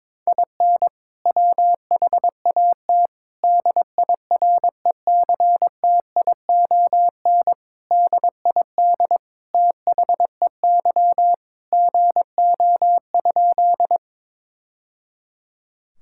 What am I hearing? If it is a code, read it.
Morse code, heard as a keyed tone, about 22 wpm, 700 Hz. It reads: IN WHAT DIRECTION DID THEY GO?
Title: Sentences from Top 1000 Words - Review Part 5 - 22wpm